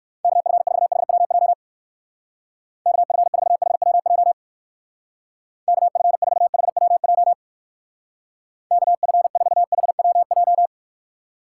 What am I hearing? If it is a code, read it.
Morse code, heard as a keyed tone, about 59 wpm, 700 Hz. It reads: KF4HWJ KF4HWJ KF4HWJ KF4HWJ